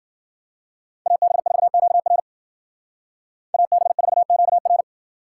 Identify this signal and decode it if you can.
Morse code, heard as a keyed tone, about 49 wpm, 700 Hz. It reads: AB4QR AB4QR
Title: Speed Racing - USA Call Signs 40wpm